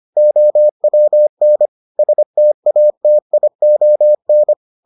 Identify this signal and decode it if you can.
Morse code, heard as a keyed tone, about 25 wpm, 600 Hz. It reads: OWN STATION